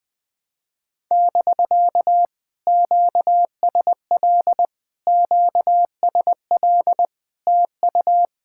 Morse code, heard as a keyed tone, about 20 wpm, 700 Hz. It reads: <BK> QSL QSL TU
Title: POTA QSO Practice - 20wpm